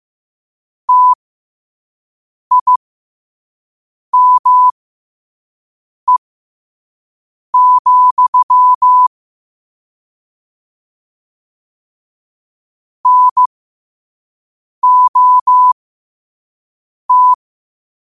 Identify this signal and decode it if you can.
Morse code, heard as a keyed tone, about 15 wpm, spaced as for 5 wpm, 1000 Hz. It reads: TIME, NOT
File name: qotdfiles / 5wpm